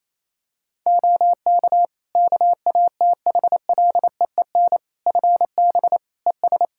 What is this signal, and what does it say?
Morse code, heard as a keyed tone, about 28 wpm, 700 Hz. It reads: OK KATHLEEN FB ES
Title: Practice QSO - Exchange 2 - Info - 28wpm